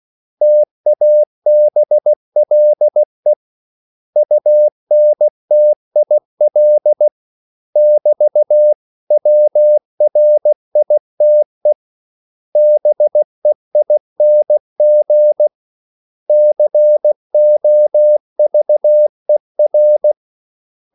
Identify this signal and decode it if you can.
Morse code, heard as a keyed tone, about 16 wpm, 600 Hz. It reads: TABLE UNTIL <BT> WRITE BEING COVER
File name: CW / Practice / 500MostCommonEnglishWords16